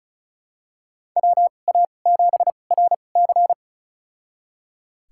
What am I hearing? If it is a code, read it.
Morse code, heard as a keyed tone, about 34 wpm, spaced as for 25 wpm, 700 Hz. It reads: WA7RC
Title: USA Call Signs - 5 Characters - 2x Character Spacing - 35wpm